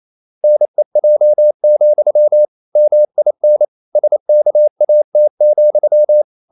Morse code, heard as a keyed tone, about 28 wpm, 600 Hz. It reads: NEJ, MIN SKAT,
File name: Ferie QRQ 28wpm